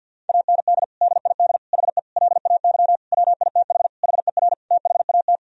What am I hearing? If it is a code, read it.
Morse code, heard as a keyed tone, about 52 wpm, 700 Hz. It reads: AND DID HE LAY WITH HER THAT
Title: Sentences from Top 1000 Words - Review Part 2 - 50wpm